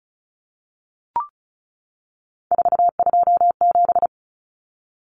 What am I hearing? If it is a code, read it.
Morse code, heard as a keyed tone, about 34 wpm, 700 Hz. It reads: 427
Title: Numbers - 3-Digit 35wpm